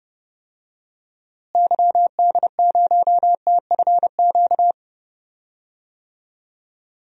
Morse code, heard as a keyed tone, about 30 wpm, 700 Hz. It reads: YD0TFQ
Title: DXCC-100 Call Signs - 6 Characters - Repeated 3x - 30wpm